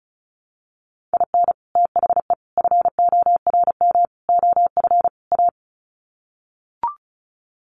Morse code, heard as a keyed tone, about 34 wpm, 700 Hz. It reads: IN THE FORM OF A
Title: Sets of 5 Words 35wpm